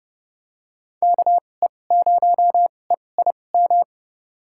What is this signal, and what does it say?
Morse code, heard as a keyed tone, about 30 wpm, spaced as for 22 wpm, 700 Hz. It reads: KE0EIM
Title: USA Call Signs - 2x Character Spacing - 30wpm